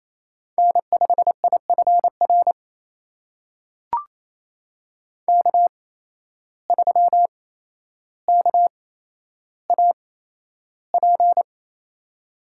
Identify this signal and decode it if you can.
Morse code, heard as a keyed tone, about 28 wpm, spaced as for 12 wpm, 700 Hz. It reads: N5IFR K3KAP
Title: USA Call Signs - 5 Characters - 8x Character Spacing - 28wpm